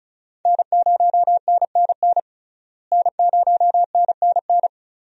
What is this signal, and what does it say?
Morse code, heard as a keyed tone, about 36 wpm, 700 Hz. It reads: N0NNN N0NNN
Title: USA Call Signs - 5 Characters - Repeated 3x - 35wpm